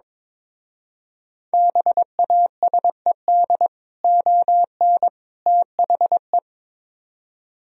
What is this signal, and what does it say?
Morse code, heard as a keyed tone, about 22 wpm, 700 Hz. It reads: BASED ON THE
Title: Speed-Racing - Sets of 3 Words 22wpm